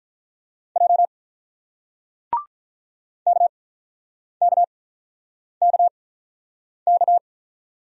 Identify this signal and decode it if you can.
Morse code, heard as a keyed tone, about 48 wpm, spaced as for 9 wpm, 700 Hz. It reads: J KKKK